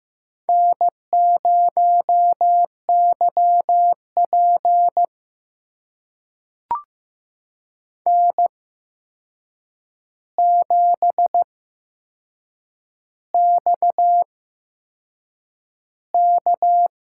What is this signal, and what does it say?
Morse code, heard as a keyed tone, about 15 wpm, spaced as for 6 wpm, 700 Hz. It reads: N0YP N7XK
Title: USA Call Signs - 4 Characters - 8x Character Spacing - 15wpm